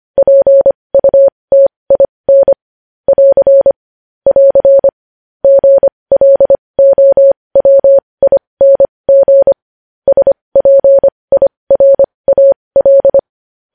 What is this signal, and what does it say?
Morse code, heard as a keyed tone, about 25 wpm, spaced as for 20 wpm, 550 Hz. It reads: PUTIN <AR> <AR> GLOWING SPIRAL
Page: news-headlines-in-morse-code-at-20-wpm